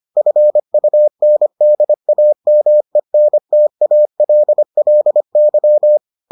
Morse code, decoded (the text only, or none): FUNDAMENTALLY